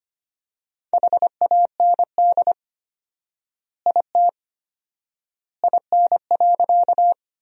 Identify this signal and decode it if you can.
Morse code, heard as a keyed tone, about 25 wpm, 700 Hz. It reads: HAND IT IN.
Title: 4x Word-Spacing - Sentences from Top 200 Words - 25wpm